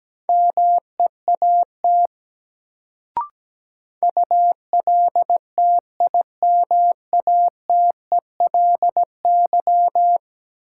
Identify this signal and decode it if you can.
Morse code, heard as a keyed tone, about 17 wpm, 700 Hz. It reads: MEAT ULTIMATELY
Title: Sentences from Top 1200 Words - New Word Drill Alt - 17wpm